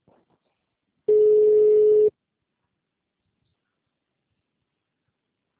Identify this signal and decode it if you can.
Ringback tone (Europe, 425 Hz).